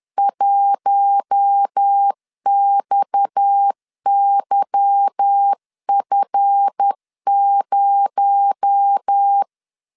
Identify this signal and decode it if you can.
Morse code, heard as a keyed tone, about 10 wpm, 800 Hz. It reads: 1XYF0
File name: LEZIONI / 8 YJ45 / 8 YJ 10wpm 2651